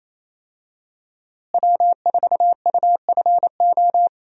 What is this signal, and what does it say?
Morse code, heard as a keyed tone, about 27 wpm, 700 Hz. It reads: W4UFO
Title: USA Call Signs - 5 Characters - Repeated 2x - 28wpm